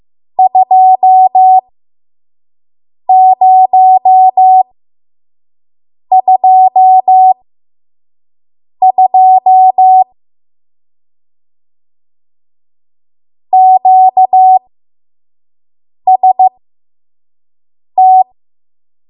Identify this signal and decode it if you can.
Morse code, heard as a keyed tone, about 15 wpm, spaced as for 5 wpm, 750 Hz. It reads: 2022 QST
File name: Morse / 05WPM